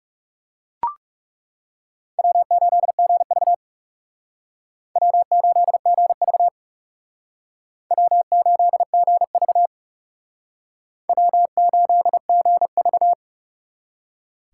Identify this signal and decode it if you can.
Morse code, heard as a keyed tone, about 36 wpm, 700 Hz. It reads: W8GV W8GV W8GV W8GV
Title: Speed Racing - USA Call Signs 30wpm